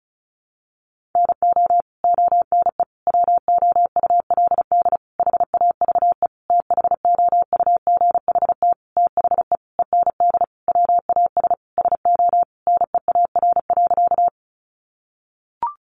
Morse code, heard as a keyed tone, about 34 wpm, 700 Hz. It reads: NO ONE WOULD HAVE THOUGHT THE END WAS SO NEAR.
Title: Sentences from Top 300 Words 35wpm